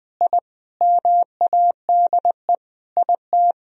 Morse code, heard as a keyed tone, about 20 wpm, 700 Hz. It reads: I MADE IT
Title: Sentences from Top 1000 Words - Review Part 2 - 20wpm